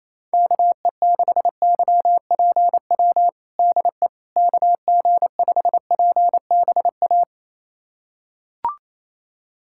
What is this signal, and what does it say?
Morse code, heard as a keyed tone, about 28 wpm, 700 Hz. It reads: KE6YPW DE KG5PBA
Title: USA Callsign DE Callsign - 6 Characters - 28wpm